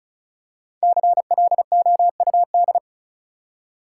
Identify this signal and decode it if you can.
Morse code, heard as a keyed tone, about 36 wpm, 700 Hz. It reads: CLOUD
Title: Five Letter Words - 35wpm